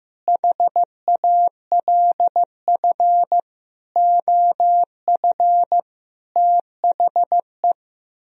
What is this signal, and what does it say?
Morse code, heard as a keyed tone, about 15 wpm, 700 Hz. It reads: HALF OF THE
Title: Speed-Racing - Sets of 5 Words 15wpm